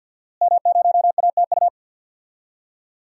Morse code, heard as a keyed tone, about 51 wpm, 700 Hz. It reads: M0ATU
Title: DXCC-100 Call Signs - 5 Characters - Repeated 3x - 50wpm